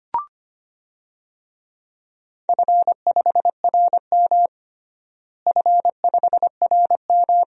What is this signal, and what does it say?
Morse code, heard as a keyed tone, about 25 wpm, 700 Hz. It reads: F5RM F5RM
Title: DXCC-100 Call Signs - 4 Characters - Repeated 3x - 25wpm